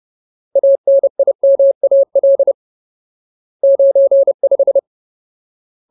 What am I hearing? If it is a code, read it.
Morse code, heard as a keyed tone, about 30 wpm, 550 Hz. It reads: ANIMAL 95
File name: Ultimate Word List / 30WPM Ultimate